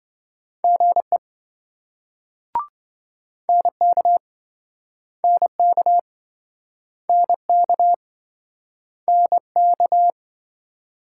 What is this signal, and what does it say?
Morse code, heard as a keyed tone, about 27 wpm, 700 Hz. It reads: GE NK NK NK NK